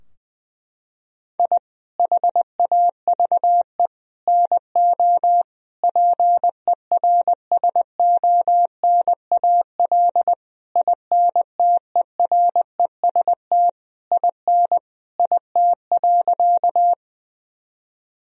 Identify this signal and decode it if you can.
Morse code, heard as a keyed tone, about 20 wpm, 700 Hz. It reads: I HAVE NO PERSONAL INTEREST IN IT.
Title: Sentences from Top 1000 Words - Review Part 4 - 20wpm